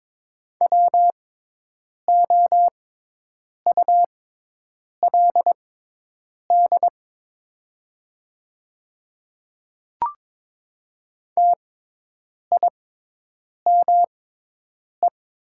Morse code, heard as a keyed tone, about 22 wpm, spaced as for 8 wpm, 700 Hz. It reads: WOULD TIME
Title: 6x Character Spacing - Top 100 Words - 22wpm